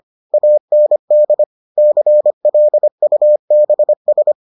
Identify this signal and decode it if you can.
Morse code, heard as a keyed tone, about 25 wpm, 600 Hz. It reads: AND CLUBS